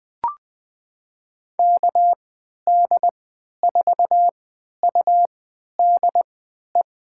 Morse code, heard as a keyed tone, about 20 wpm, spaced as for 11 wpm, 700 Hz. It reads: KD4UDE
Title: USA Call Signs - 3x Character Spacing - 20wpm